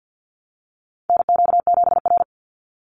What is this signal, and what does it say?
Morse code, heard as a keyed tone, about 50 wpm, 700 Hz. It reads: NQ7R
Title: NA SOTA Chaser Callsigns 50wpm